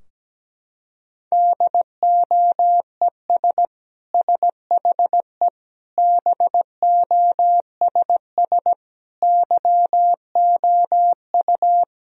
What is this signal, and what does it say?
Morse code, heard as a keyed tone, about 17 wpm, 700 Hz. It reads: DOES SHE BOSS YOU
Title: Sentences from Top 1000 Words - Review Part 4 - 17wpm